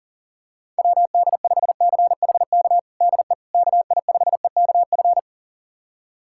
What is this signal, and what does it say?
Morse code, heard as a keyed tone, about 40 wpm, 700 Hz. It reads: WD5CHK DE KI5EKF
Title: USA Callsign DE Callsign 40wpm